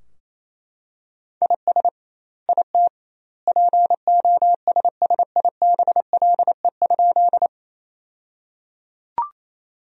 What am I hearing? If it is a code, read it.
Morse code, heard as a keyed tone, about 28 wpm, 700 Hz. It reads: IS IT POSSIBLE?
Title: 2x Word-Spacing - Sentences from Top 500 Words - 28wpm